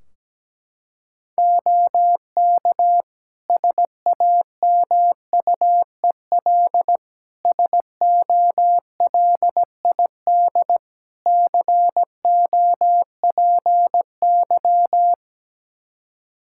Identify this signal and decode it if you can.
Morse code, heard as a keyed tone, about 17 wpm, 700 Hz. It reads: OK SAMUEL SOLID COPY
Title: K1USN SST QSO Practice 17wpm